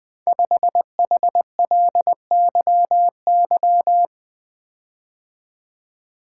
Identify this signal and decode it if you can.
Morse code, heard as a keyed tone, about 20 wpm, 700 Hz. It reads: 5HLYY
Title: Code Groups - 5 Letter:Numbers - 20wpm